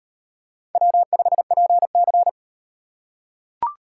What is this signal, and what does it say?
Morse code, heard as a keyed tone, about 38 wpm, 700 Hz. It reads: W5PC